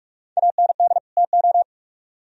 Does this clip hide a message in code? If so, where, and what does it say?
Yes: Morse code from 0.4 to 1.6 s, heard as a keyed tone, about 44 wpm, 700 Hz. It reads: AND TO